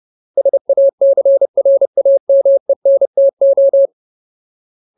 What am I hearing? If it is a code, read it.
Morse code, heard as a keyed tone, about 30 wpm, 550 Hz. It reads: SACRAMENTO
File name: Ultimate Word List / 30WPM Ultimate